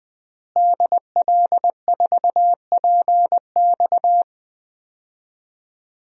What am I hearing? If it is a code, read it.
Morse code, heard as a keyed tone, about 20 wpm, 700 Hz. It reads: DL4PX
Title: DXCC-100 Call Signs - 5 Characters - Repeated 3x - 20wpm